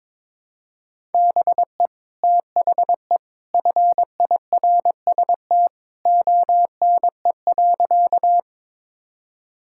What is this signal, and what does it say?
Morse code, heard as a keyed tone, about 22 wpm, 700 Hz. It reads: BE THE FIRST ONE.
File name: practice / 22wpm / Sentences with 2-Word Phrases - 22wpm